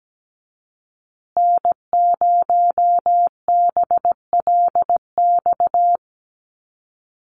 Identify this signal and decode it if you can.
Morse code, heard as a keyed tone, about 17 wpm, 700 Hz. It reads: N0BLX
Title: USA Call Signs Repeated 2x 17wpm